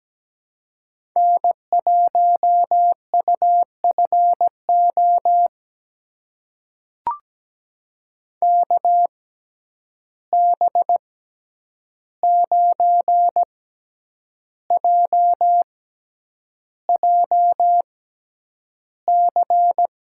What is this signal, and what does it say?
Morse code, heard as a keyed tone, about 17 wpm, spaced as for 8 wpm, 700 Hz. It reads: N1UFO KB9JJC
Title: USA Call Signs - 6x Character Spacing - 17wpm